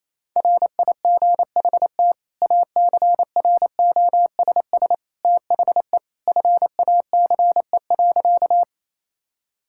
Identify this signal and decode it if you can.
Morse code, heard as a keyed tone, about 28 wpm, 700 Hz. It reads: RIGHT ACROSS THE FACE.
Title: Sentences from Top 1000 Words - Review Part 5 - 28wpm